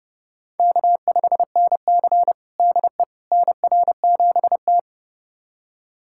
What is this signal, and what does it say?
Morse code, heard as a keyed tone, about 30 wpm, 700 Hz. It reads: K5NC DE NR7T
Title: USA Callsign DE Callsign - 4 Characters - 30wpm